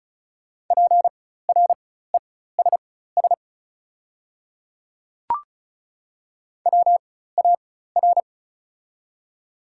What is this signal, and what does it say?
Morse code, heard as a keyed tone, about 36 wpm, spaced as for 16 wpm, 690 Hz. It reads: PRESS WAR
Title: 4x Character Spacing - Top 500 Words - 35wpm